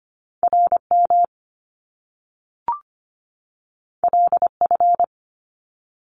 Morse code, heard as a keyed tone, about 25 wpm, 700 Hz. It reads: RM LF
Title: Top Two Letter Combinations in Words 25wpm